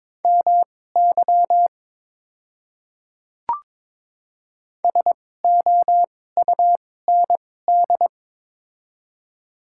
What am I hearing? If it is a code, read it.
Morse code, heard as a keyed tone, about 22 wpm, spaced as for 16 wpm, 700 Hz. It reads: MY SOUND
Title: 2x Character Spacing - Top 200 Words - 22wpm